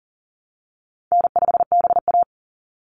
Morse code, heard as a keyed tone, about 40 wpm, 700 Hz. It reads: N5BA
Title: Easy Call Signs Repeated 3x - 40wpm